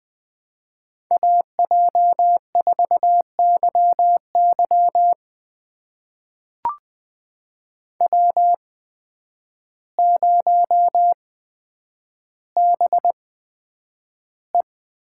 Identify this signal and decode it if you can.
Morse code, heard as a keyed tone, about 20 wpm, spaced as for 9 wpm, 700 Hz. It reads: AJ4YY W0BE
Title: USA Call Signs - 8x Character Spacing - 20wpm